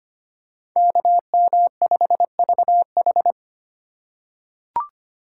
Morse code, heard as a keyed tone, about 25 wpm, 700 Hz. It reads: KM5VH
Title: USA Call Signs - 5 Characters - 3x Character Spacing - 25wpm